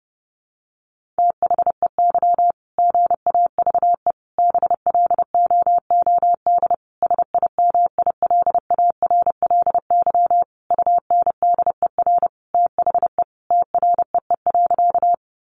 Morse code, heard as a keyed tone, about 28 wpm, 700 Hz. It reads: THEY GAVE BLOOD SIMILARLY UNDER THE TREE.
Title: Sentences from Top 1400 Words 30wpm